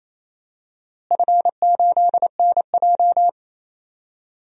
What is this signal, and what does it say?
Morse code, heard as a keyed tone, about 28 wpm, 700 Hz. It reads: F8NJ
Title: DXCC-100 Call Signs - 4 Characters - Repeated 3x - 28wpm